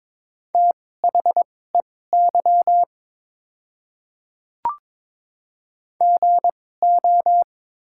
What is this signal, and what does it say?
Morse code, heard as a keyed tone, about 22 wpm, spaced as for 16 wpm, 700 Hz. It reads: THEY GO
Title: 2x Character Spacing - Top 200 Words - 22wpm